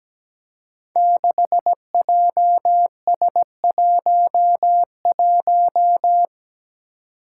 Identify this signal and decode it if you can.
Morse code, heard as a keyed tone, about 17 wpm, 700 Hz. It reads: 6JS11
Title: Five Challenging Letter-Number Combinations 17wpm